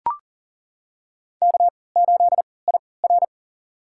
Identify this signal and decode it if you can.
Morse code, heard as a keyed tone, about 40 wpm, spaced as for 23 wpm, 700 Hz. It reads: K8IR
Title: USA Call Signs - 4 Characters - 3x Character Spacing - 40wpm